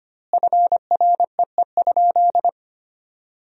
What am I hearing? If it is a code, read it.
Morse code, heard as a keyed tone, about 25 wpm, 700 Hz. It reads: FREE?
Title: 2x Word-Spacing - Sentences from Top 500 Words - 25wpm